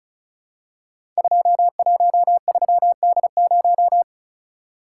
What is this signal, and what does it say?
Morse code, heard as a keyed tone, about 36 wpm, 700 Hz. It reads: 213D0
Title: Code Groups - 5 Letter:Numbers - 35wpm